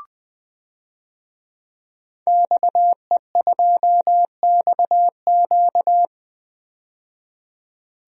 Morse code, heard as a keyed tone, about 20 wpm, 700 Hz. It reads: XE2XQ
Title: DXCC-100 Call Signs - 5 Characters - 20wpm